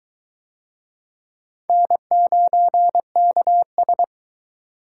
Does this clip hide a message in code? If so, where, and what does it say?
Yes: Morse code from 1.7 to 4.0 s, heard as a keyed tone, about 22 wpm, 700 Hz. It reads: N9KS